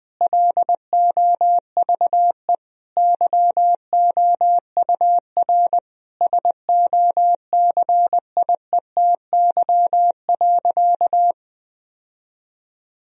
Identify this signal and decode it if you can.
Morse code, heard as a keyed tone, about 20 wpm, 700 Hz. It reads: LOVE YOUR SOCIETY.
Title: Sentences from Top 1000 Words - Review Part 4 - 20wpm